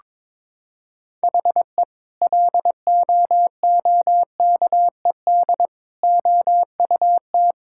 Morse code, heard as a keyed tone, about 22 wpm, 700 Hz. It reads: HE LOOKED OUT
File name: practice / 22wpm / Sentences with 2-Word Phrases - 22wpm